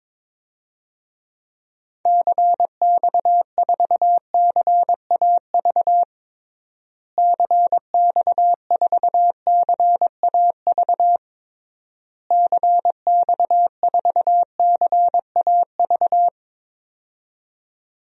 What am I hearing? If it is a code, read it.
Morse code, heard as a keyed tone, about 22 wpm, 700 Hz. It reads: CX4CAV CX4CAV CX4CAV
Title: DXCC-100 Call Signs - 6 Characters - Repeated 3x - 22wpm